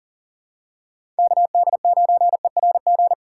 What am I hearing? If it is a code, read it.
Morse code, heard as a keyed tone, about 40 wpm, 700 Hz. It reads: KD9ERG
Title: USA Call Signs - 6 Characters - 6x Character Spacing - 40wpm